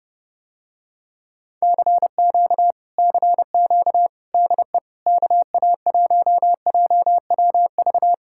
Morse code, heard as a keyed tone, about 31 wpm, 700 Hz. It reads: CQ CQ DE KA1JWV